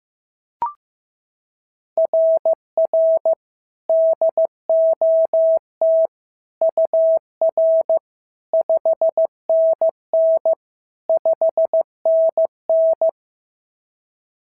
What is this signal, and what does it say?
Morse code, heard as a keyed tone, about 14 wpm, 650 Hz. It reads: RR DOT UR 5NN 5NN
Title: SOTA QSO Practice 15wpm